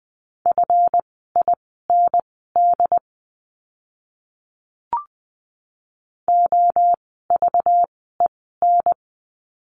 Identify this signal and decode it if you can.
Morse code, heard as a keyed tone, about 20 wpm, spaced as for 15 wpm, 700 Hz. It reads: FIND OVEN
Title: 192 Lesson 31f - Letter v - Words 20wpm